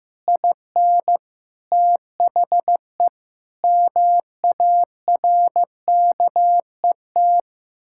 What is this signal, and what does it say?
Morse code, heard as a keyed tone, about 15 wpm, 700 Hz. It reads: IN THE MARKET
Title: Sentences from Top 1600 Words 15wpm